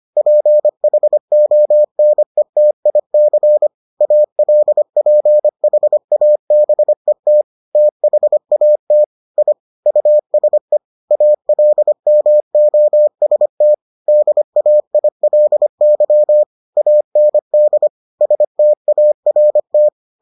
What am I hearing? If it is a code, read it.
Morse code, heard as a keyed tone, about 25 wpm, 600 Hz. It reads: PHONETIC ALPHABET THAT I USE ALMOST DAILY AND START